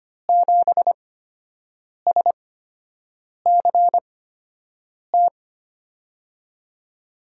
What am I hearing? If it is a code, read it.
Morse code, heard as a keyed tone, about 25 wpm, spaced as for 7 wpm, 700 Hz. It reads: 7SCT
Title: USA Call Signs - 5 Characters - 8x Character Spacing - 25wpm